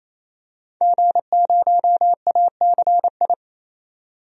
Morse code, heard as a keyed tone, about 28 wpm, 700 Hz. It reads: G0ACI
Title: DXCC-100 Call Signs - 5 Characters - 28wpm